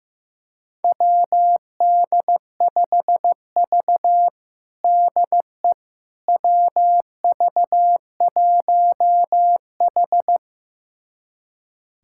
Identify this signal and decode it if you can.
Morse code, heard as a keyed tone, about 15 wpm, 700 Hz. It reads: WD5V DE WV1H
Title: USA Callsign DE Callsign - 4 Characters - 15wpm